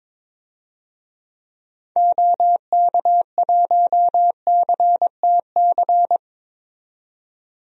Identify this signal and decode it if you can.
Morse code, heard as a keyed tone, about 22 wpm, 700 Hz. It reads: OK1CTC
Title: DXCC-100 Call Signs - 6 Characters - 22wpm